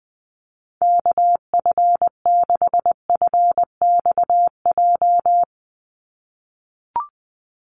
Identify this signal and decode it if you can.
Morse code, heard as a keyed tone, about 20 wpm, 700 Hz. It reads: KF6FXJ
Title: USA Call Signs 20wpm